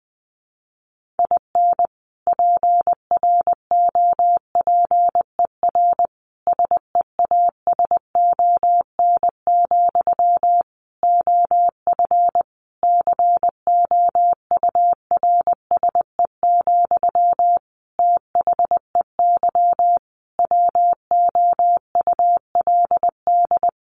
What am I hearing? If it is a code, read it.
Morse code, heard as a keyed tone, about 20 wpm, 700 Hz. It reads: IN PROPER SEASON, OF COURSE, THEY WOULD
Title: Sentences from Top 1400 Words 20wpm